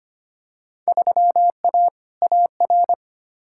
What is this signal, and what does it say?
Morse code, heard as a keyed tone, about 25 wpm, 700 Hz. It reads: 3A AR
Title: Field Day Exchanges Encore - 25wpm